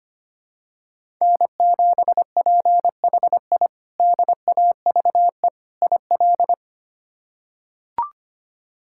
Morse code, heard as a keyed tone, about 25 wpm, 700 Hz. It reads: N7PHI DAVE IL
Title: K1USN SST QSO Practice - 25wpm